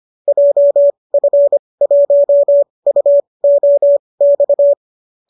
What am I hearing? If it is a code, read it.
Morse code, heard as a keyed tone, about 25 wpm, spaced as for 20 wpm, 590 Hz. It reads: JF1UOX